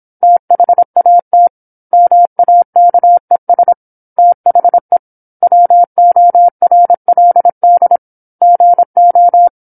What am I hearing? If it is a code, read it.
Morse code, heard as a keyed tone, about 27 wpm, 700 Hz. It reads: THAT MAKES THE WORLD GO